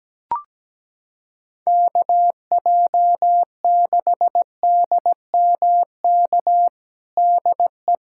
Morse code, heard as a keyed tone, about 17 wpm, 700 Hz. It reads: KJ6DMK DE
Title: USA Callsign DE Callsign - 6 Characters - 17wpm